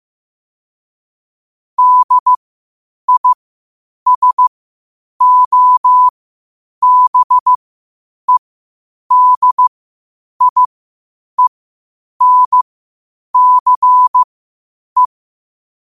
Morse code, heard as a keyed tone, about 15 wpm, spaced as for 8 wpm, 1000 Hz. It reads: DISOBEDIENCE
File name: qotdfiles / 8wpm